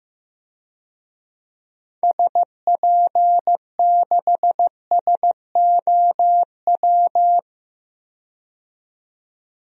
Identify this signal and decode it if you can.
Morse code, heard as a keyed tone, about 15 wpm, 700 Hz. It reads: SP6SOW
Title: DXCC-100 Call Signs - 6 Characters - 15wpm